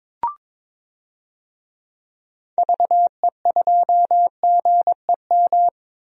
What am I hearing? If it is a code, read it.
Morse code, heard as a keyed tone, about 22 wpm, 700 Hz. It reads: VE2GEM
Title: DXCC-100 Call Signs - 6 Characters - 22wpm